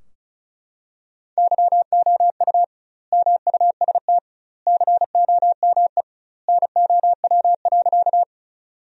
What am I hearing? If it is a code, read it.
Morse code, heard as a keyed tone, about 34 wpm, 700 Hz. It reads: YOU MUST COME NOW.
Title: 2x Word-Spacing - Sentences from Top 200 Words - 35wpm